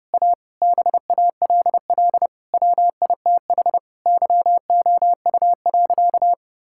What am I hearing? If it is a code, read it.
Morse code, heard as a keyed tone, about 30 wpm, 700 Hz. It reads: A BALL WITH YOU.
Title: Sentences from Top 1000 Words - Review Part 6 - 30wpm